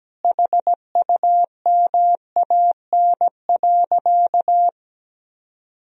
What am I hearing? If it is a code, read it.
Morse code, heard as a keyed tone, about 17 wpm, 700 Hz. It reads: HUMAN.
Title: Sentences from Top 1000 Words - Review Part 3 - 17wpm